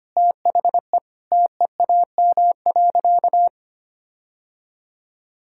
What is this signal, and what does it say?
Morse code, heard as a keyed tone, about 25 wpm, 700 Hz. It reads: THE TEAM.
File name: practice / 25wpm / Sentences with 3-Word Phrases - 25wpm